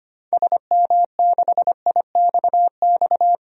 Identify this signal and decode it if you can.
Morse code, heard as a keyed tone, about 25 wpm, 700 Hz. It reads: SM6IXX
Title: DXCC-100 Call Signs - 6 Characters - 25wpm